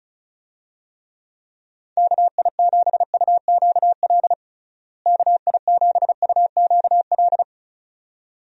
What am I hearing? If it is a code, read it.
Morse code, heard as a keyed tone, about 36 wpm, 700 Hz. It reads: KI7UQL KI7UQL
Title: USA Call Signs - 6 Characters - Repeated 2x - 35wpm